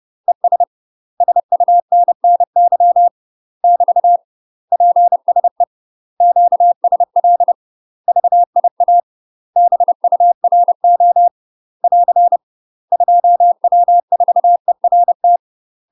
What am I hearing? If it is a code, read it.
Morse code, heard as a keyed tone, about 30 wpm, 700 Hz. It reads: ES SUNNY <BT> PSE QSL VIA BURO <AR> 2W4ERT
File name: UK MASTER QSOs TRAINING / 30WPM / UK MASTER QSOs 0002